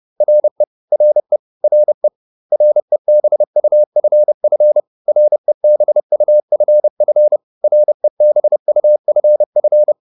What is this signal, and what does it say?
Morse code, heard as a keyed tone, about 30 wpm, 600 Hz. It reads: RE RE RE REBUFF REBUFF REBUFF